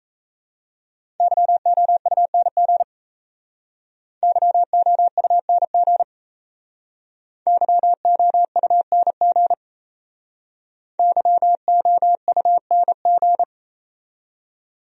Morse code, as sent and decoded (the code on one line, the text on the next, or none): -.-- --- ..- -. --. / -.-- --- ..- -. --. / -.-- --- ..- -. --. / -.-- --- ..- -. --.
YOUNG YOUNG YOUNG YOUNG